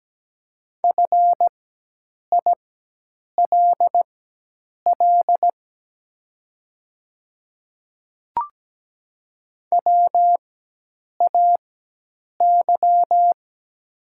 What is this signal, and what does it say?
Morse code, heard as a keyed tone, about 17 wpm, spaced as for 8 wpm, 700 Hz. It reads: FILL WAY
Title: 4x Character Spacing - Top 500 Words - 17wpm